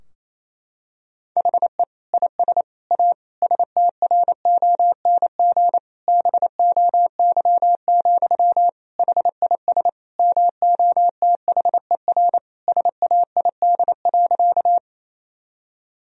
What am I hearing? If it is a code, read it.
Morse code, heard as a keyed tone, about 27 wpm, 700 Hz. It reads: HE IS A STRONG BOY, HIS MOTHER SAID.
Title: Sentences from Top 1000 Words - Review Part 3 - 28wpm